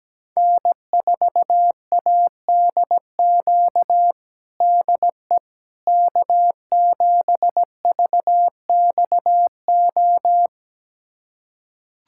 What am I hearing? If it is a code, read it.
Morse code, heard as a keyed tone, about 17 wpm, 700 Hz. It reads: N4ADQ DE K7VXO
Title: USA Callsign DE Callsign - 5 Characters - 17wpm